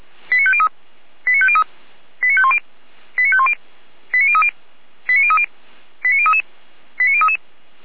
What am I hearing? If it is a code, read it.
Five-tone selective call (ZVEI), heard as quick runs of five tones, tones of about 70 ms. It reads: 89503 89503 89510 89510 89030 89030 89033 89033